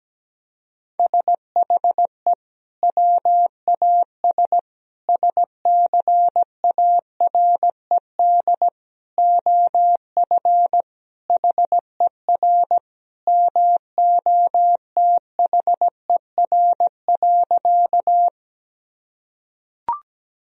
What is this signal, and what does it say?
Morse code, heard as a keyed tone, about 17 wpm, 700 Hz. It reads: SHE WAS SCARED OF HER MOTHER.
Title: Sentences from Top 1000 Words - Review Part 4 - 17wpm